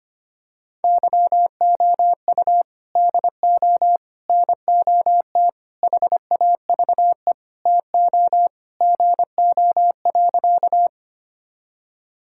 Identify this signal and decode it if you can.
Morse code, heard as a keyed tone, about 25 wpm, 700 Hz. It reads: YOU DO NOT HAVE TO GO.
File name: practice / 25wpm / Sentences with 4-Word Phrases - 25wpm